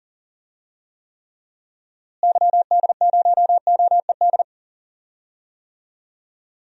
Morse code, heard as a keyed tone, about 40 wpm, 700 Hz. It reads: YD0OED
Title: DXCC-100 Call Signs - 6 Characters - Repeated 3x - 40wpm